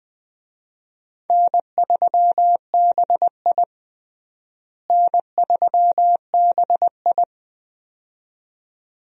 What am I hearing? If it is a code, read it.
Morse code, heard as a keyed tone, about 20 wpm, 700 Hz. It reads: N3BI N3BI
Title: USA Call Signs - 4 Characters - Repeated 2x - 20wpm